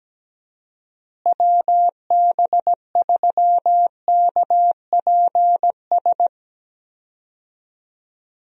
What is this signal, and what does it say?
Morse code, heard as a keyed tone, about 17 wpm, 700 Hz. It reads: WB3KPS
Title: USA Call Signs - 6 Characters - Repeated 2x - 17wpm